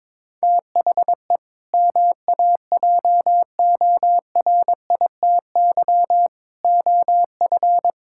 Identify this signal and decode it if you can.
Morse code, heard as a keyed tone, about 22 wpm, 700 Hz. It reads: THE MAJORITY OF
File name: practice / 22wpm / Sentences with 3-Word Phrases - 22wpm